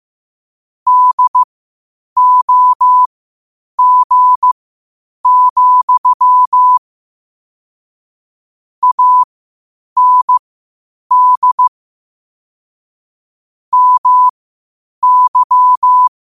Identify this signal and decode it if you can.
Morse code, heard as a keyed tone, about 15 wpm, spaced as for 8 wpm, 1000 Hz. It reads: DOG, AND MY
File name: qotdfiles / 8wpm